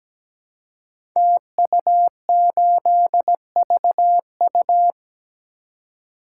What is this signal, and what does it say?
Morse code, heard as a keyed tone, about 17 wpm, 700 Hz. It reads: TU8VU
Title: Code Groups - 5 Letter:Numbers - 17wpm